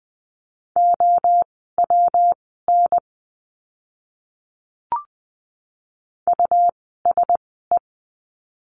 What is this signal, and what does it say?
Morse code, heard as a keyed tone, about 20 wpm, spaced as for 15 wpm, 700 Hz. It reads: OWN USE
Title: 192 Lesson 31f - Letter v - Words 20wpm